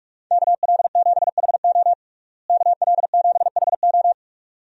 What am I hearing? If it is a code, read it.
Morse code, heard as a keyed tone, about 46 wpm, 700 Hz. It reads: KL7HO KL7HO
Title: DXCC-100 Call Signs - 5 Characters - Repeated 2x - 45wpm